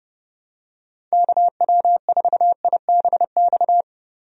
Morse code, heard as a keyed tone, about 30 wpm, 700 Hz. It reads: KW4IBX
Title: USA Call Signs - 6 Characters - Repeated 3x - 30wpm